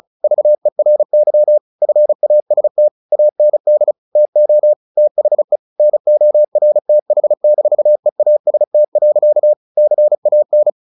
Morse code, heard as a keyed tone, about 34 wpm, 600 Hz. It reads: VERY FAST AND TO THE NORTH-EAST. CAN
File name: the old man and the sea / the old man and the sea 1 35wpm